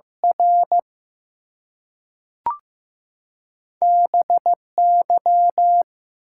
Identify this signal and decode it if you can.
Morse code, heard as a keyed tone, about 15 wpm, 700 Hz. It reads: R BY
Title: Two Letter Words - 15wpm